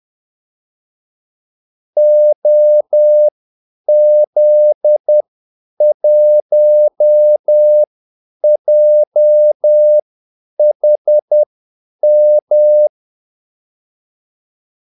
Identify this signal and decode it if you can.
Morse code, heard as a keyed tone, about 10 wpm, spaced as for 8 wpm, 600 Hz. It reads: OZ1JHM
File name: Callsigns-10-8wpm